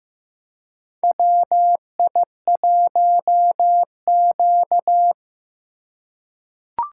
Morse code, heard as a keyed tone, about 15 wpm, 700 Hz. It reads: WI1Q
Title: USA Call Signs - 4 Characters - 2x Character Spacing - 15wpm